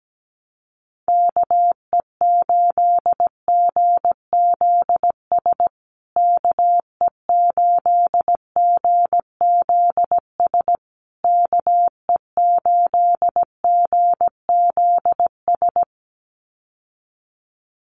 Morse code, heard as a keyed tone, about 17 wpm, 700 Hz. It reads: KE8GZS KE8GZS KE8GZS
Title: USA Call Signs Repeated 3x 17wpm